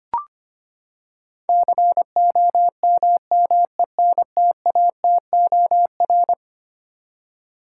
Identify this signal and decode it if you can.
Morse code, heard as a keyed tone, about 25 wpm, 700 Hz. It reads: COMMENTATOR
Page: morse-code-ninja-practice-25wpm